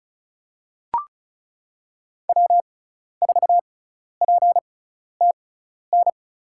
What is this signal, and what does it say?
Morse code, heard as a keyed tone, about 34 wpm, spaced as for 12 wpm, 700 Hz. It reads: W4PTN
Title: USA Call Signs - 6x Character Spacing - 35wpm